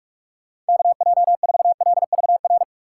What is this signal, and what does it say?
Morse code, heard as a keyed tone, about 46 wpm, 700 Hz. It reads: KJ4LVR
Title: USA Call Signs - 6 Characters - 4x Character Spacing - 45wpm